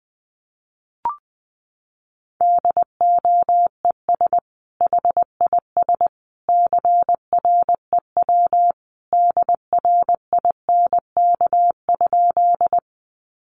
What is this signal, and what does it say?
Morse code, heard as a keyed tone, about 19 wpm, 700 Hz. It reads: DOES HIS CREW DRINK?
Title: Sentences from Top 1500 Words 20wpm